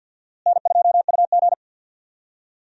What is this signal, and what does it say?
Morse code, heard as a keyed tone, about 53 wpm, 700 Hz. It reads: N2UG
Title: USA Call Signs - 4 Characters - Repeated 3x - 50wpm